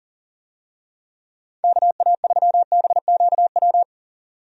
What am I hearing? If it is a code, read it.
Morse code, heard as a keyed tone, about 40 wpm, 700 Hz. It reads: KA3BQW
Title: USA Call Signs - 6 Characters - 40wpm